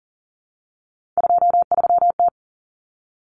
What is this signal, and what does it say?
Morse code, heard as a keyed tone, about 37 wpm, 700 Hz. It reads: 23T
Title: Common Cut Numbers - 3-Digit 40wpm